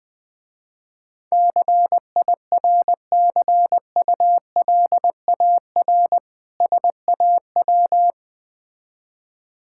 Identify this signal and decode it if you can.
Morse code, heard as a keyed tone, about 20 wpm, 700 Hz. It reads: CIRCULAR SAW
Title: Difficult Words with Hints - 20wpm